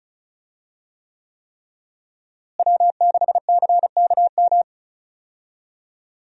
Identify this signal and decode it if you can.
Morse code, heard as a keyed tone, about 36 wpm, 700 Hz. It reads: W6CKM
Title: USA Call Signs - 5 Characters - Repeated 3x - 35wpm